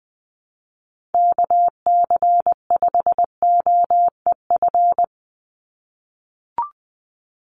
Morse code, heard as a keyed tone, about 20 wpm, 700 Hz. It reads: KC5OEF
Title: USA Call Signs 20wpm